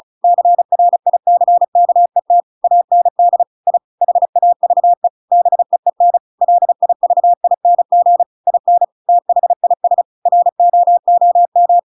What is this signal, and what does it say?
Morse code, heard as a keyed tone, about 34 wpm, 700 Hz. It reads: CRICKET AND I HAVE BEEN LIVING IN THIS ROOM